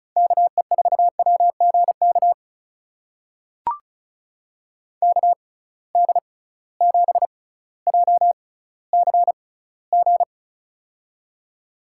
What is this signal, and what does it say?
Morse code, heard as a keyed tone, about 36 wpm, spaced as for 18 wpm, 700 Hz. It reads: KE4WGK KD7JCG
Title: USA Call Signs - 6 Characters - 6x Character Spacing - 35wpm